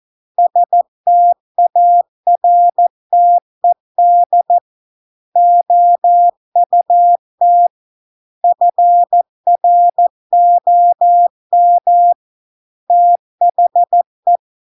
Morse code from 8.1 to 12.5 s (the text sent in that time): FROM